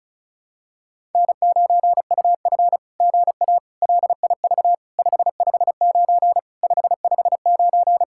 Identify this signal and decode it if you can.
Morse code, heard as a keyed tone, about 36 wpm, 700 Hz. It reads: N9UF GA LIV 559 559
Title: SOTA QSO Practice 35wpm